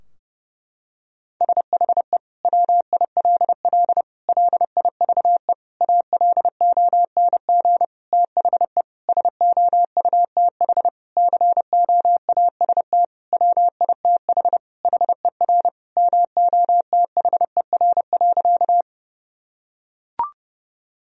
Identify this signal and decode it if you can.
Morse code, heard as a keyed tone, about 30 wpm, 700 Hz. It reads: SHE WILL LIVE ALONG THE SOUTH COAST WITH HER MOTHER.
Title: Sentences from Top 1000 Words - Review Part 4 - 30wpm